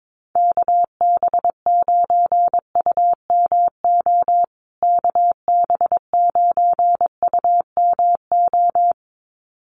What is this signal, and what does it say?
Morse code, heard as a keyed tone, about 22 wpm, 700 Hz. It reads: KB9UMO KB9UMO
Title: USA Call Signs Repeated 2x 22wpm